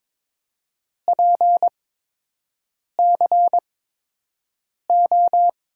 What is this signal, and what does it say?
Morse code, heard as a keyed tone, about 22 wpm, spaced as for 6 wpm, 700 Hz. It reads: PCO
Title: USA Call Signs - 6 Characters - 8x Character Spacing - 22wpm